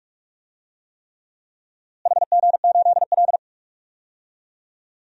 Morse code, heard as a keyed tone, about 45 wpm, 700 Hz. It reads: HG8L
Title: DXCC-100 Call Signs - 4 Characters - Repeated 3x - 45wpm